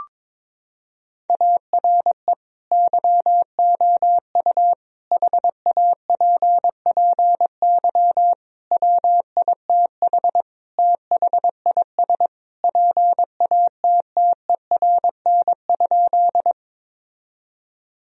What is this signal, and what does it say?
Morse code, heard as a keyed tone, about 23 wpm, 700 Hz. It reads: ARE YOU HAPPY WITH THIS PATTERN?
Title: Sentences from Top 1000 Words - Review Part 2 - 22wpm